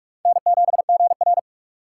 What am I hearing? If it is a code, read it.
Morse code, heard as a keyed tone, about 60 wpm, spaced as for 53 wpm, 700 Hz. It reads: N7GR
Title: Speed Racing - USA 1x2 Call Signs 45wpm